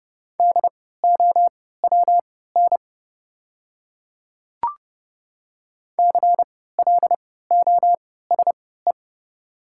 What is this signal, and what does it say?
Morse code, heard as a keyed tone, about 30 wpm, spaced as for 17 wpm, 690 Hz. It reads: DOWN CLOSE
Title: 3x Character Spacing - Top 500 Words - 30wpm